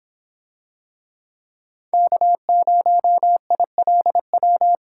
Morse code, heard as a keyed tone, about 23 wpm, 700 Hz. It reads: K0ILW